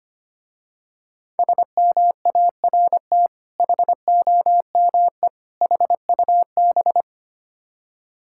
Morse code, heard as a keyed tone, about 25 wpm, 700 Hz. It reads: SMART HOME HUB
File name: practice / 25wpm / Difficult Words with Hints - 25wpm